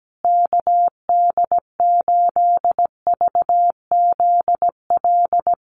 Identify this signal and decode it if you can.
Morse code, heard as a keyed tone, about 17 wpm, 700 Hz. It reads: KD8VZL
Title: USA Call Signs Repeated 3x 17wpm